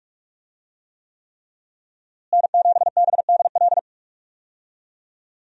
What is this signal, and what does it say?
Morse code, heard as a keyed tone, about 46 wpm, 700 Hz. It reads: N7BDL
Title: USA Call Signs - 5 Characters - Repeated 3x - 45wpm